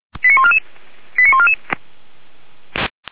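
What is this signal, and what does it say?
Five-tone selective call (ZVEI), heard as quick runs of five tones, tones of about 70 ms. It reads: 89144 89144
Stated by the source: ZVEI five-tone selective call 89155 received by SDR.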